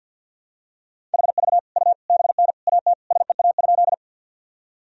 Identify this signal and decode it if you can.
Morse code, heard as a keyed tone, about 47 wpm, 700 Hz. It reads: HV U BN AT SEA?
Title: Sentences from Top 300 Words and Abbreviations 50wpm